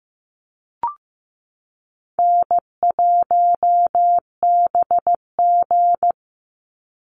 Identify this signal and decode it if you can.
Morse code, heard as a keyed tone, about 15 wpm, 700 Hz. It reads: N1BG
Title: Easy Call Signs - 15wpm